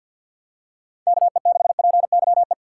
Morse code, heard as a keyed tone, about 48 wpm, 700 Hz. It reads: KE6PYE
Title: USA Call Signs - 6 Characters - 4x Character Spacing - 50wpm